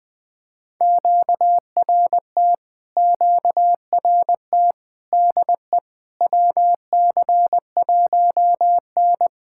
Morse code, heard as a keyed tone, about 20 wpm, 700 Hz. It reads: QRT QRT DE WC1N
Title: POTA QSO Practice - 20wpm